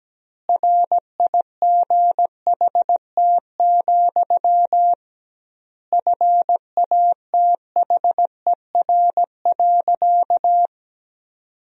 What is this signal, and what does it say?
Morse code, heard as a keyed tone, about 17 wpm, 700 Hz. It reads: RIGHT, FATHER.
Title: 2x Word-Spacing - Sentences from Top 300 Words - 17wpm